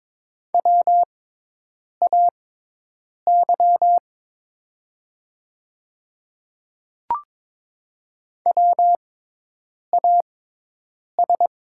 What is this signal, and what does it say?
Morse code, heard as a keyed tone, about 23 wpm, spaced as for 8 wpm, 700 Hz. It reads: WAY WAS
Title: 6x Character Spacing - Top 100 Words - 22wpm